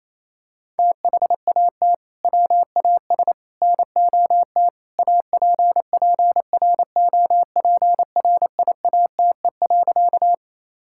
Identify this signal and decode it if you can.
Morse code, heard as a keyed tone, about 28 wpm, 700 Hz. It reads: THAT WAS NOT APPROPRIATE.
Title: Sentences from Top 1000 Words - Review Part 5 - 28wpm